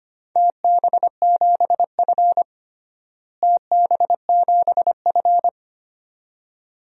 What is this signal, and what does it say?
Morse code, heard as a keyed tone, about 25 wpm, 700 Hz. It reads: TB7F TB7F
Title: DXCC-100 Call Signs - 4 Characters - Repeated 2x - 25wpm